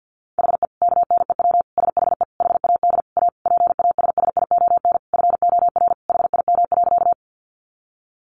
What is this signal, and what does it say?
Morse code, heard as a keyed tone, about 50 wpm, 700 Hz. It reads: HE KNEW SHE HAD A PASSION FOR HIM.
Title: Sentences from Top 1300 Words 50wpm